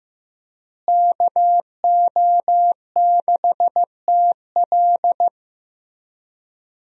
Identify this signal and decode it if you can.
Morse code, heard as a keyed tone, about 15 wpm, 700 Hz. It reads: KO6TL
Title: USA Call Signs - 5 Characters - Repeated 2x - 15wpm